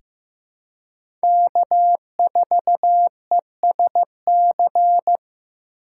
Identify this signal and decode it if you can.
Morse code, heard as a keyed tone, about 15 wpm, 700 Hz. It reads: K4ESC
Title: USA Call Signs - 5 Characters - 4x Character Spacing - 15wpm